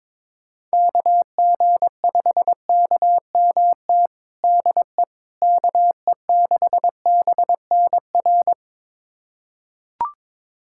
Morse code, heard as a keyed tone, about 22 wpm, 700 Hz. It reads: KG5KMT DE KE6BNR
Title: USA Callsign DE Callsign - 6 Characters - 22wpm